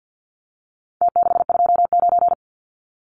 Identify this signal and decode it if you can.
Morse code, heard as a keyed tone, about 48 wpm, 700 Hz. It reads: T629
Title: Numbers - Contest Style - 4-Digit 50wpm